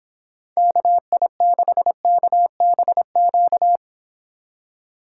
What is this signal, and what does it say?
Morse code, heard as a keyed tone, about 29 wpm, 700 Hz. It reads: KI6KBQ